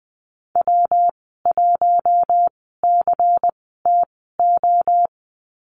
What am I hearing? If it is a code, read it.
Morse code, heard as a keyed tone, about 20 wpm, spaced as for 14 wpm, 700 Hz. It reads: W1CTO
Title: 122 Lesson 20g - Number 3 - Call Signs 20wpm